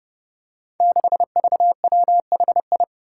Morse code, heard as a keyed tone, about 30 wpm, 700 Hz. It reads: 6VWHI
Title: Five Challenging Letter-Number Combinations 30wpm